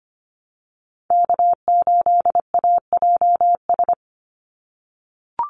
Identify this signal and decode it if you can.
Morse code, heard as a keyed tone, about 25 wpm, 700 Hz. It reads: K8AJS
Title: Call Signs - 25wpm